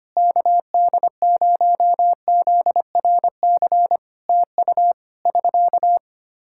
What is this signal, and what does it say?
Morse code, heard as a keyed tone, about 25 wpm, 700 Hz. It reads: KD0ZRC TU <SK>
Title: K1USN SST QSO Practice 25wpm